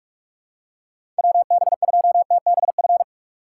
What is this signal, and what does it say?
Morse code, heard as a keyed tone, about 44 wpm, 700 Hz. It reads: WB2TBF
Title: USA Call Signs - 3x Character Spacing - 45wpm